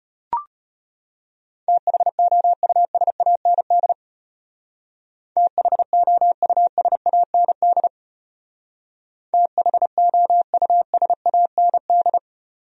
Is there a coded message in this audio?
Yes: Morse code from 1.7 to 12.2 s, heard as a keyed tone, about 32 wpm, 700 Hz. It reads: THOUSAND THOUSAND THOUSAND